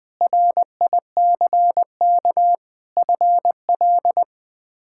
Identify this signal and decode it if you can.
Morse code, heard as a keyed tone, about 20 wpm, 700 Hz. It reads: RICK FL
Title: K1USN SST QSO Practice - 20wpm